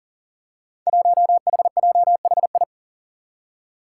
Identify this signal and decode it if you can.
Morse code, heard as a keyed tone, about 40 wpm, 700 Hz. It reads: 1HJHI
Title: Five Challenging Letter-Number Combinations 40wpm